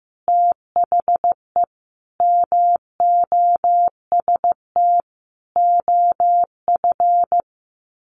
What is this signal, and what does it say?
Morse code, heard as a keyed tone, about 15 wpm, 700 Hz. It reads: THE MOST OF
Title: Sets of 5 Words Encore2 15wpm